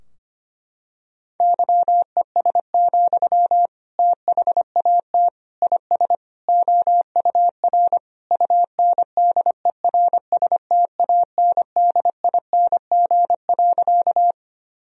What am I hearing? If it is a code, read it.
Morse code, heard as a keyed tone, about 25 wpm, 700 Hz. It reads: YES, THAT IS OUR UNDERSTANDING.
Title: Sentences from Top 1000 Words - Review Part 3 - 25wpm